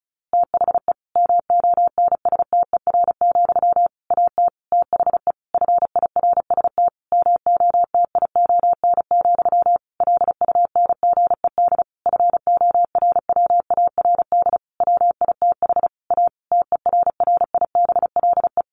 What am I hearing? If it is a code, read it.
Morse code, heard as a keyed tone, about 34 wpm, 700 Hz. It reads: THE MONSTER, AT THE FIRST MOTION, LUNGED FORWARD WITH A TERRIBLE
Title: A Sound Of Thunder 35wpm